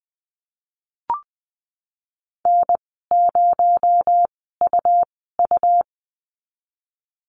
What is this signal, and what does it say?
Morse code, heard as a keyed tone, about 20 wpm, spaced as for 15 wpm, 700 Hz. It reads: N0UU
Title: 219 Lesson 35g - Number 0 - Call Signs 20wpm